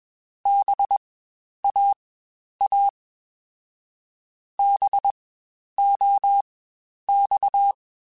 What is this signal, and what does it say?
Morse code, heard as a keyed tone, about 21 wpm, spaced as for 10 wpm, 800 Hz. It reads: BAA BOX
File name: Spaced Alphas / Spaced Alphas mp3s / Bs sp @20